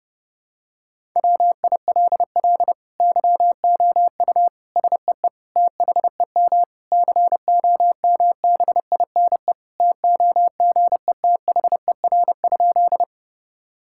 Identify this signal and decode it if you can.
Morse code, heard as a keyed tone, about 30 wpm, 700 Hz. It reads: WILL YOU SEE THEM COMBINE TOGETHER?
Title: Sentences from Top 1600 Words 30wpm